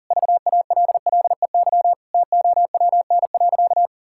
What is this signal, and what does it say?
Morse code, heard as a keyed tone, about 40 wpm, 700 Hz. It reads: VALLEY TOWN.
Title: Pangram Challenge v2 - 40wpm